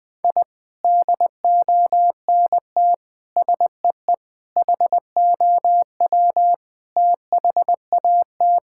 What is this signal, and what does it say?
Morse code, heard as a keyed tone, about 20 wpm, 700 Hz. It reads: I DONT SEE HOW THAT
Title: Sentences from Top 1000 Words - Review Part 6 - 20wpm